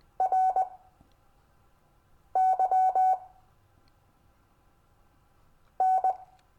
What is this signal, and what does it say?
Morse code, heard as a keyed tone, about 20 wpm, spaced as for 5 wpm, 750 Hz. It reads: RY N
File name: cw / Lesson15 five forwardslash